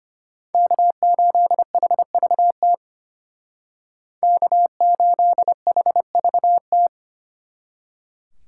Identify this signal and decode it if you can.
Morse code, heard as a keyed tone, about 33 wpm, 700 Hz. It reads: K8HVT K8HVT